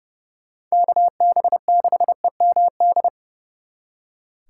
Morse code, heard as a keyed tone, about 37 wpm, 700 Hz. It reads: KB6EMD
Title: Speed Racing - USA 2x3 Call Signs 30wpm